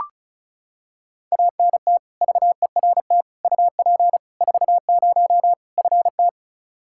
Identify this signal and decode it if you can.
Morse code, heard as a keyed tone, about 34 wpm, 700 Hz. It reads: ANT VERT UP 40 FT
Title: Practice QSO - Exchange 2 - Info - 35wpm